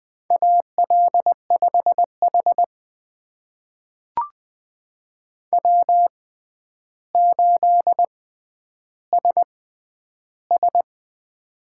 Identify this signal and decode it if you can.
Morse code, heard as a keyed tone, about 20 wpm, spaced as for 10 wpm, 700 Hz. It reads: AL5H W8SS
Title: USA Call Signs - 4 Characters - 6x Character Spacing - 20wpm